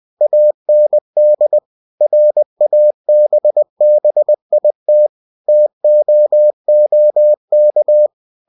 Morse code, heard as a keyed tone, about 20 wpm, 600 Hz. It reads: AND RABBIT TOOK
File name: cw-books / winnie-the-pooh / winnie-20wpm-page-0019